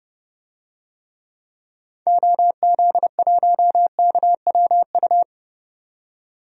Morse code, heard as a keyed tone, about 30 wpm, 700 Hz. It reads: OZ1KWU
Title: DXCC-100 Call Signs - 6 Characters - 30wpm